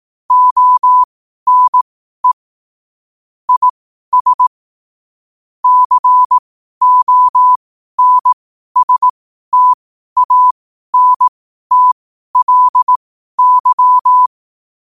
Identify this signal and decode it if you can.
Morse code, heard as a keyed tone, about 18 wpm, spaced as for 12 wpm, 1000 Hz. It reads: ONE IS CONSTANTLY
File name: qotdfiles / 12wpm